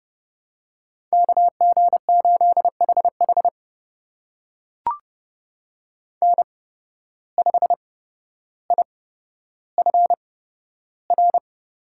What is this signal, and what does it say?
Morse code, heard as a keyed tone, about 31 wpm, spaced as for 13 wpm, 700 Hz. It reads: KG8HH N5IFR
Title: USA Call Signs - 5 Characters - 8x Character Spacing - 30wpm